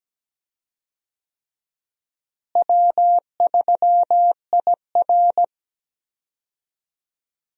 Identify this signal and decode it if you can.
Morse code, heard as a keyed tone, about 17 wpm, 700 Hz. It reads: W3IR
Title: USA Call Signs - 4 Characters - Repeated 3x - 17wpm